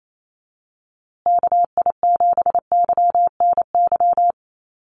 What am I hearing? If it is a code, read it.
Morse code, heard as a keyed tone, about 28 wpm, 690 Hz. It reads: KI7YNY
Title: USA Call Signs Repeated 3x 28wpm